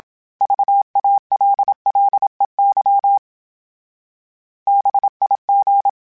Morse code, heard as a keyed tone, about 27 wpm, 800 Hz. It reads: VALLEY BIG
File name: fromSMs / FromSMs mp3s / Cities@25